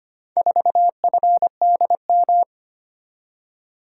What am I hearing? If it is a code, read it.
Morse code, heard as a keyed tone, about 25 wpm, 700 Hz. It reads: 4FDM
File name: practice / 25wpm / Four Letter-Number Combinations - 25wpm